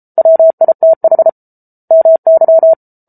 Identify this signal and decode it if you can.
Morse code, heard as a keyed tone, about 34 wpm, 650 Hz. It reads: WITH MY